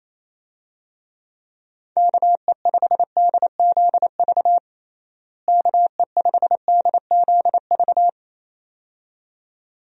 Morse code, heard as a keyed tone, about 28 wpm, 700 Hz. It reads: KE5DZV KE5DZV
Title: USA Call Signs - 6 Characters - Repeated 2x - 28wpm